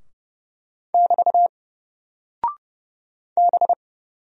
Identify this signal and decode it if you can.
Morse code, heard as a keyed tone, about 30 wpm, 700 Hz. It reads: <BT> B
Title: Single Character (Letter:Number:Punctuation:Prosign) Alt 30wpm